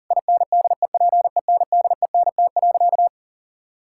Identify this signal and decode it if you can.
Morse code, heard as a keyed tone, about 40 wpm, 700 Hz. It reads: INDEPENDENT.
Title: Sentences from Top 1000 Words - Review Part 6 - 40wpm